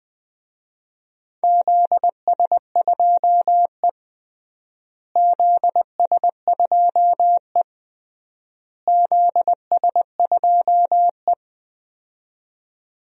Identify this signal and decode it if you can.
Morse code, heard as a keyed tone, about 19 wpm, 700 Hz. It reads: ZS2E ZS2E ZS2E
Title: DXCC-100 Call Signs - 4 Characters - Repeated 3x - 20wpm